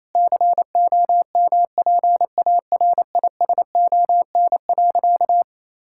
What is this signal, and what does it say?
Morse code, heard as a keyed tone, about 28 wpm, 700 Hz. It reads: COMPARISON.
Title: Sentences from Top 1600 Words 28wpm